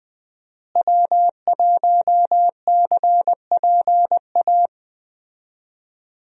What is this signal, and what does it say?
Morse code, heard as a keyed tone, about 20 wpm, 700 Hz. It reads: W1CPA
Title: USA Call Signs - 5 Characters - Repeated 2x - 20wpm